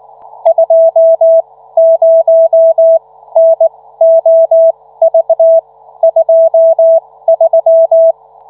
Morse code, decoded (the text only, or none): 20NOV23